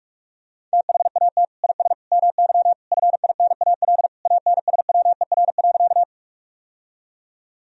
Morse code, heard as a keyed tone, about 46 wpm, 700 Hz. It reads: THAT IS MY FINAL ANSWER.
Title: Sentences from Top 1000 Words - Review Part 2 - 45wpm